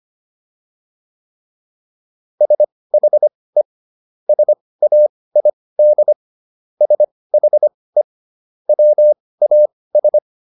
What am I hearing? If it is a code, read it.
Morse code, heard as a keyed tone, about 25 wpm, spaced as for 18 wpm, 600 Hz. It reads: SHE SAID SHE WAS